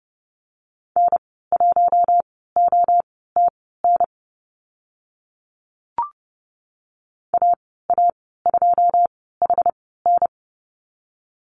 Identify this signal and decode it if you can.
Morse code, heard as a keyed tone, about 30 wpm, spaced as for 17 wpm, 700 Hz. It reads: N1OTN AA2HN
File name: Lesson 14g - Number 2 - Call Signs 30wpm